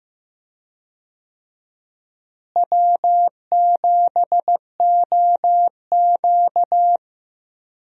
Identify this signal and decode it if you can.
Morse code, heard as a keyed tone, about 15 wpm, 700 Hz. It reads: W7OQ
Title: USA Call Signs - 4 Characters - Repeated 3x - 15wpm